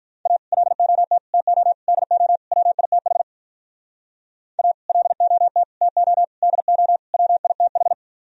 Morse code, heard as a keyed tone, about 45 wpm, 700 Hz. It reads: A LOT TO DO WITH A LOT TO DO WITH